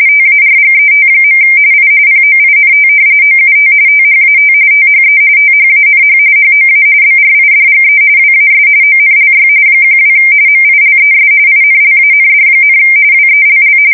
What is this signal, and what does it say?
RTTY, about 45.45 baud. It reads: TARGET. I POINT OUT THAT THE THANH HOA BRIDGE WAS ONLY 12 MILES INLAND, WELL
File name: fsk170 / battleshipnj